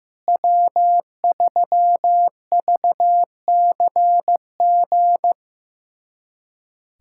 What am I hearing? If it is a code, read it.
Morse code, heard as a keyed tone, about 16 wpm, 700 Hz. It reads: W3VCG
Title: USA Callsign DE Callsign 15wpm